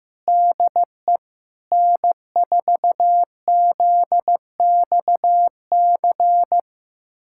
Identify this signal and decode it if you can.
Morse code, heard as a keyed tone, about 15 wpm, 700 Hz. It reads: DE N4ZXC
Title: USA Callsign DE Callsign - 5 Characters - 15wpm